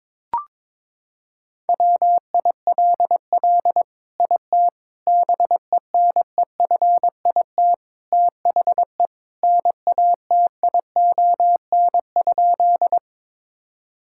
Morse code, heard as a keyed tone, about 22 wpm, 700 Hz. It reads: WILL IT BENEFIT THE NATION?
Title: Sentences from Top 1000 Words - Review Part 6 - 22wpm